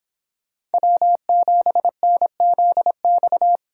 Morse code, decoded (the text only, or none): W7NZX